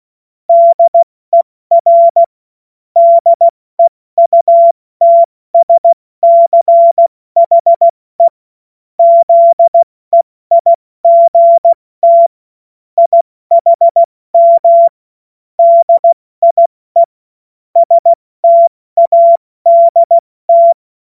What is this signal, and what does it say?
Morse code, heard as a keyed tone, about 16 wpm, 680 Hz. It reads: DER DEUTSCHE ZEIGT IHM DIE STADT
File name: Morsekurs / Witze / Witz 07.November 2023 70 F80